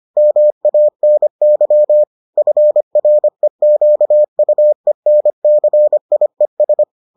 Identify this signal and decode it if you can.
Morse code, heard as a keyed tone, about 25 wpm, 600 Hz. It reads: MANY FREQUENCIES